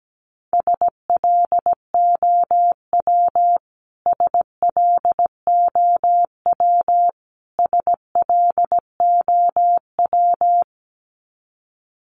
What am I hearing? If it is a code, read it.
Morse code, heard as a keyed tone, about 17 wpm, 700 Hz. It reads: SLOW SLOW SLOW
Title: Top 500 Words Repeated 17wpm